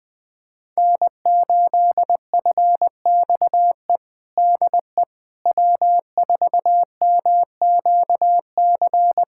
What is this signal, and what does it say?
Morse code, heard as a keyed tone, about 21 wpm, 700 Hz. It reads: N8FXE DE W4MQC
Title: USA Callsign DE Callsign 20wpm